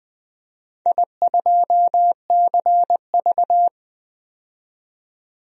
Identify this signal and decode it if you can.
Morse code, heard as a keyed tone, about 20 wpm, 700 Hz. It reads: I2CV
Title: DXCC-100 Call Signs - 4 Characters - 20wpm